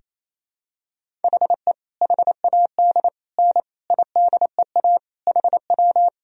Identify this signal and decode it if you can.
Morse code, heard as a keyed tone, about 28 wpm, 700 Hz. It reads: HE HAD N IDEA HW
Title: Sentences from Top 300 Words and Abbreviations 28wpm